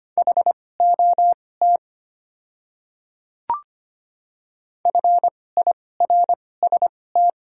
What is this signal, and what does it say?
Morse code, heard as a keyed tone, about 25 wpm, spaced as for 18 wpm, 700 Hz. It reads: HOT FIRST
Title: 2x Character Spacing - Top 100 Words - 25wpm